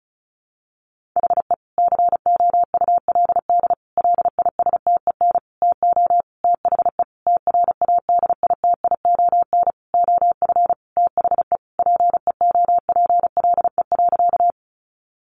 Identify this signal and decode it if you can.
Morse code, heard as a keyed tone, about 34 wpm, 700 Hz. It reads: HE COULD LISTEN TO THE TRADITION OF THE PEOPLE.
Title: Sentences from Top 1500 Words 35wpm